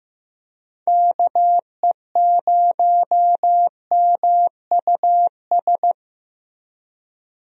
Morse code, heard as a keyed tone, about 15 wpm, 700 Hz. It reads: KE0MUS
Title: USA Call Signs - 6 Characters - Repeated 3x - 15wpm